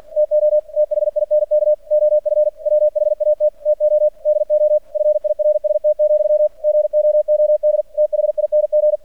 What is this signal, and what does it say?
Morse code, heard as a keyed tone, about 46 wpm, 600 Hz. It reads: TO THINK OF WHAT TO DO FIRST, GOOD THING
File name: archived / CRSnet